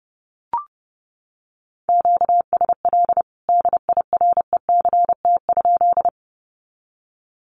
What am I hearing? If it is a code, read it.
Morse code, heard as a keyed tone, about 30 wpm, 700 Hz. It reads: QSL DIRECT?
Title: Common QSO Phrases 30wpm